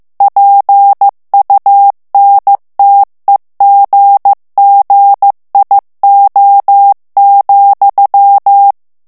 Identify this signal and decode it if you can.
Morse code, heard as a keyed tone, about 15 wpm, 800 Hz. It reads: PUNTEGGIO,
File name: morse mp3 / ITALIEN texte clair 2 vitesse 14 mots minute